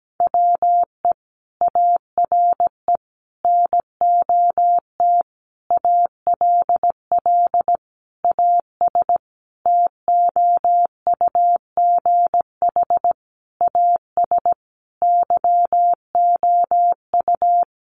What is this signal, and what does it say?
Morse code, heard as a keyed tone, about 17 wpm, 700 Hz. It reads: WE ARE NOT ALL AS TOUGH AS YOU
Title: Sentences from Top 1200 Words 17wpm